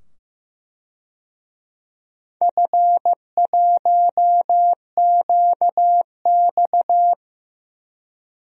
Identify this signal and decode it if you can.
Morse code, heard as a keyed tone, about 15 wpm, 700 Hz. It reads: F1QX
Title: DXCC-100 Call Signs - 4 Characters - 15wpm